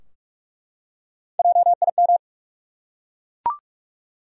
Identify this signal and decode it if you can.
Morse code, heard as a keyed tone, about 43 wpm, 700 Hz. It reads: JIM
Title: USA Common Names - Max 4 Letters - 45wpm